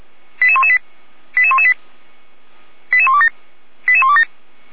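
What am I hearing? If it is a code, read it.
Five-tone selective call (ZVEI), heard as quick runs of five tones, tones of about 70 ms. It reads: 88118 88118 88127 88127
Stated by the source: Five-tone ZVEI selective call received by SDR, logged on server of Feuerwehr Orlamünde (fire brigade).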